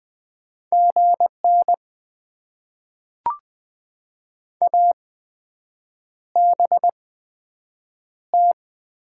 Morse code, heard as a keyed tone, about 20 wpm, spaced as for 7 wpm, 700 Hz. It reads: GN ABT
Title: Top 90 QSO Elements - 8x Character Spacing - 20wpm